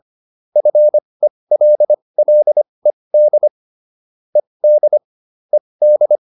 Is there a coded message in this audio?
Yes: Morse code from 0.6 to 6.2 s, heard as a keyed tone, about 25 wpm, spaced as for 20 wpm, 600 Hz. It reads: FELLED ED ED